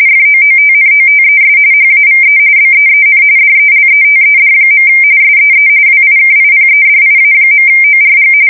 RTTY, about 45.45 baud. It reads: FOOTNOTES HAVE BEEN DELETED. SEE WIKIPEDIA.ORG
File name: fsk170 / usnavy4